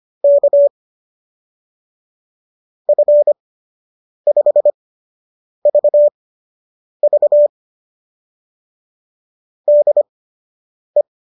Morse code, heard as a keyed tone, about 25 wpm, spaced as for 8 wpm, 590 Hz. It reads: K F5VV DE